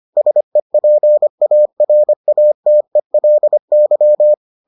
Morse code, heard as a keyed tone, about 25 wpm, 600 Hz. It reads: SEPARATELY